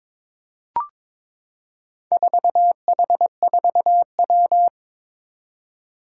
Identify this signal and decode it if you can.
Morse code, heard as a keyed tone, about 22 wpm, 700 Hz. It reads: 4H4W
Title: Four Challenging Letter-Number Combinations 22wpm